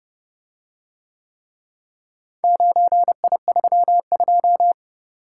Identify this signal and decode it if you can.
Morse code, heard as a keyed tone, about 29 wpm, 700 Hz. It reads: 9I32